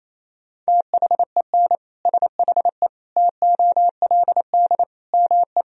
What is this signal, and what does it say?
Morse code, heard as a keyed tone, about 28 wpm, 700 Hz. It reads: THEN SHE TOLD ME
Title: Sentences from Top 400 Words and Abbreviations 28wpm